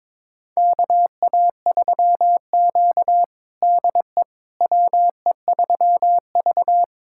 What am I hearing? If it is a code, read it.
Morse code, heard as a keyed tone, about 22 wpm, 700 Hz. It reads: KA3Q DE WE3V
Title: USA Callsign DE Callsign - 4 Characters - 22wpm